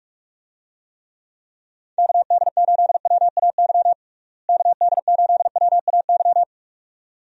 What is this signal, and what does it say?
Morse code, heard as a keyed tone, about 41 wpm, 700 Hz. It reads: KD8WAY KD8WAY
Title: USA Call Signs - 6 Characters - Repeated 2x - 45wpm